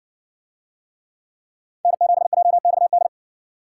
Morse code, heard as a keyed tone, about 55 wpm, 700 Hz. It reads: N7JXD